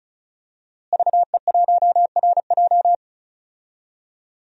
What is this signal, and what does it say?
Morse code, heard as a keyed tone, about 36 wpm, 700 Hz. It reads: VE1RJ
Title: DXCC-100 Call Signs - 5 Characters - Repeated 3x - 35wpm